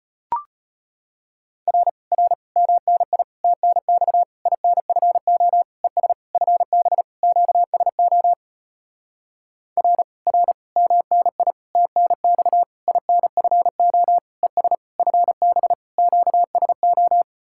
Morse code, heard as a keyed tone, about 33 wpm, 700 Hz. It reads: R R MNI TNX INFO ES FB QSO R R MNI TNX INFO ES FB QSO